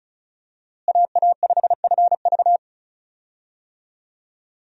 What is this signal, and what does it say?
Morse code, heard as a keyed tone, about 36 wpm, 700 Hz. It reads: AA5FV
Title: USA Call Signs - 5 Characters - Repeated 2x - 35wpm